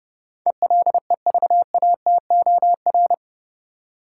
Morse code, heard as a keyed tone, about 30 wpm, 700 Hz. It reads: ELEVATOR
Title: Sentences from Top 1600 Words - New Word Drill - 30wpm